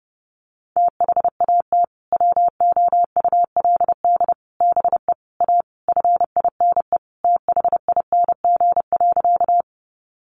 Morse code, heard as a keyed tone, about 30 wpm, 700 Hz. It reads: THAT WOULD BE A FINE THING.
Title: Sentences from Top 500 Words 30wpm